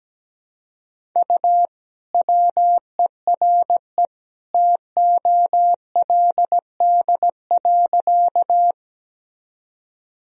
Morse code, heard as a keyed tone, about 17 wpm, 700 Hz. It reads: U WERE TOLD.
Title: Sentences from Top 400 Words and Abbreviations 17wpm